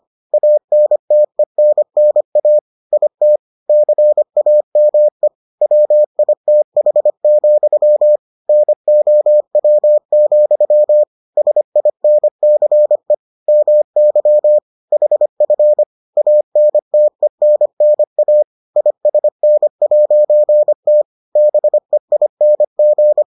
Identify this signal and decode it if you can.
Morse code, heard as a keyed tone, about 25 wpm, 600 Hz. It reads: ANTENNA IT CAME WITH, NOW, SINCE MY HF ANTENNA ISN'T BEING